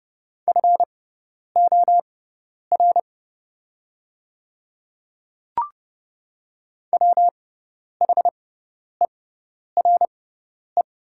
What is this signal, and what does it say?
Morse code, heard as a keyed tone, about 30 wpm, spaced as for 10 wpm, 700 Hz. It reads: FOR WHERE
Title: 6x Character Spacing - Top 200 Words - 30wpm